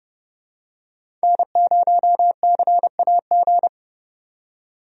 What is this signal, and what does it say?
Morse code, heard as a keyed tone, about 31 wpm, 700 Hz. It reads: N0CAG
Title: USA Call Signs - 5 Characters - 30wpm